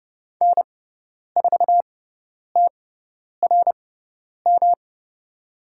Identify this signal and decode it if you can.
Morse code, heard as a keyed tone, about 30 wpm, spaced as for 10 wpm, 700 Hz. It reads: N4TRM
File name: Lesson 18g - Letter m - Call Signs 30wpm